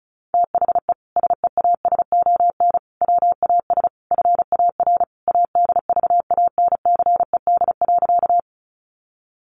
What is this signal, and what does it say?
Morse code, heard as a keyed tone, about 34 wpm, 700 Hz. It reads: THE SEASON WAS FAR ADVANCED.
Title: Sentences from Top 1400 Words 35wpm